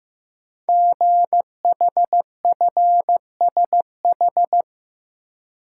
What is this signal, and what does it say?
Morse code, heard as a keyed tone, about 15 wpm, 700 Hz. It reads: GHFSH
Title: Code Groups - 5 Letters - 15wpm